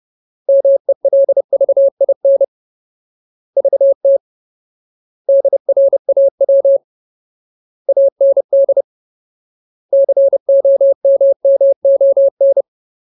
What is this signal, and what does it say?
Morse code, heard as a keyed tone, about 30 wpm, 550 Hz. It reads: MELVIN VT DRAW AND COMMON
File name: Ultimate Word List / 30WPM Ultimate